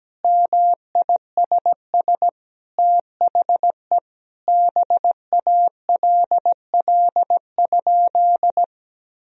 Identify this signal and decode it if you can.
Morse code, heard as a keyed tone, about 17 wpm, 700 Hz. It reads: MISS THE BALL?
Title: Sentences from Top 1000 Words - Review Part 3 - 17wpm